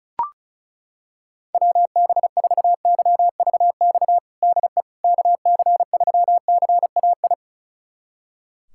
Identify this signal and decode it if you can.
Morse code, heard as a keyed tone, about 34 wpm, 700 Hz. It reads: WB4YVX DE KC3CAI
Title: USA Callsign DE Callsign - 6 Characters - 35wpm